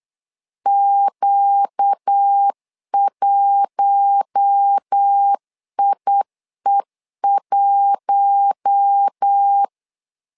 Morse code, heard as a keyed tone, about 8 wpm, 800 Hz. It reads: Q1IE1
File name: LEZIONI / 4 Q1 8wpm 3035